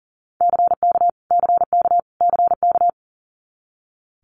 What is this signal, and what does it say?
Morse code, heard as a keyed tone, about 40 wpm, 700 Hz. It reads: CK CK CK
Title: Multi-letter Phonemes Repeated 3x 40wpm